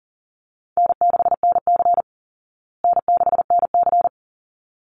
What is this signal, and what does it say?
Morse code, heard as a keyed tone, about 40 wpm, 700 Hz. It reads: N6NC N6NC
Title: Easy Call Signs Repeated 2x - 40wpm